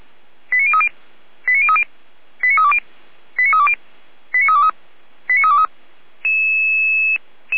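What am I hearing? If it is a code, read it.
Five-tone selective call (ZVEI), heard as quick runs of five tones, tones of about 70 ms. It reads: 89030 89030 89320 89320 89323 89323
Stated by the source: ZVEI five-tone selective call received on an SDR.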